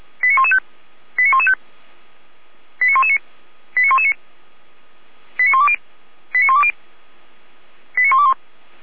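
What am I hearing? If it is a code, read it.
Five-tone selective call (ZVEI), heard as quick runs of five tones, tones of about 70 ms. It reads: 89115 89115 89119 89119 89120 89120 89121
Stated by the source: ZVEI five-tone call received by SDR.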